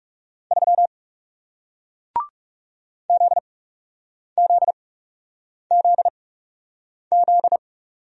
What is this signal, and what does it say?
Morse code, heard as a keyed tone, about 40 wpm, spaced as for 8 wpm, 700 Hz. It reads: 3 ZZZZ